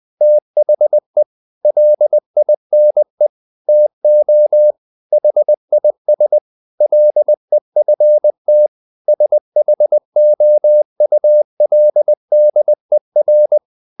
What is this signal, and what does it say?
Morse code, heard as a keyed tone, about 20 wpm, 600 Hz. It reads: THE LINE TO HIS LEFT SHOULDER
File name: the old man and the sea / the old man and the sea part 1 20wpm